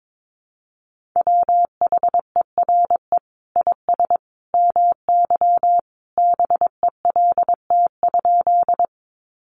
Morse code, heard as a keyed tone, about 22 wpm, 700 Hz. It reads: WHERE IS MY BELT?
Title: Sentences from Top 1400 Words 22wpm